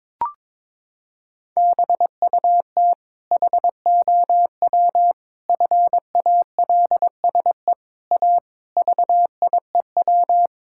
Morse code, heard as a keyed tone, about 22 wpm, 700 Hz. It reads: BUT HOW FALSE A VIEW
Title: Sentences from Top 1600 Words 22wpm